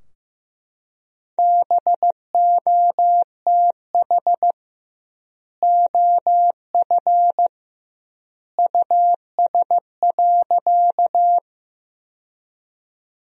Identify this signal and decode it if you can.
Morse code, heard as a keyed tone, about 15 wpm, 700 Hz. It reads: BOTH OF US.
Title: 2x Word-Spacing - Sentences from Top 300 Words - 15wpm